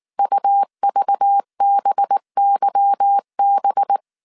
Morse code, heard as a keyed tone, about 19 wpm, 800 Hz. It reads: UVBYB
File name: CW / DET / 1 9 1500C S B 18wpm 2551